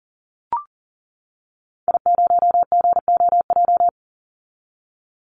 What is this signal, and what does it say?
Morse code, heard as a keyed tone, about 40 wpm, 700 Hz. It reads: I0GOJ
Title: Call Signs - 40wpm